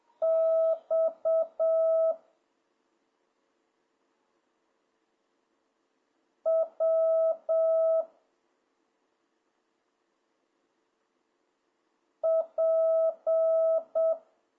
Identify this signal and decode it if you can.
Morse code, heard as a keyed tone, about 7 wpm, 650 Hz. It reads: X W P